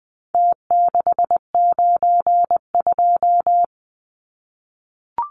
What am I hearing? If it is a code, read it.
Morse code, heard as a keyed tone, about 20 wpm, 700 Hz. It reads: T692
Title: Numbers - Contest Style - 4-Digit 20wpm